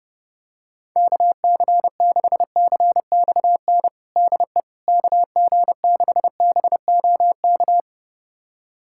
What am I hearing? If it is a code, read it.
Morse code, heard as a keyed tone, about 30 wpm, 700 Hz. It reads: KC6CXN DE KG6BOK
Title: USA Callsign DE Callsign - 6 Characters - 30wpm